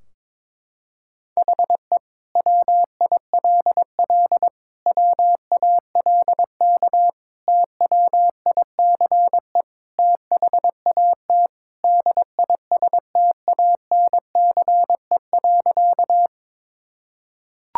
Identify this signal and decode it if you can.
Morse code, heard as a keyed tone, about 22 wpm, 700 Hz. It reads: HE WILL WALK TWICE THAT DISTANCE.
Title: Sentences from Top 1600 Words 22wpm